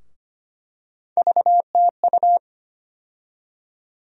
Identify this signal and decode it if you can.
Morse code, heard as a keyed tone, about 25 wpm, 700 Hz. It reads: VTU
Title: USA Call Sign Postfixes - 3 Letters 25wpm